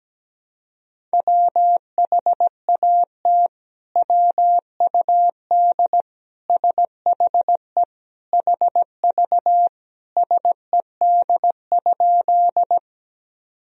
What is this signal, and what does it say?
Morse code, heard as a keyed tone, about 17 wpm, 700 Hz. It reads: WHAT WUD SHE HV SED?
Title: Sentences from Top 100 Words and Abbreviations 17wpm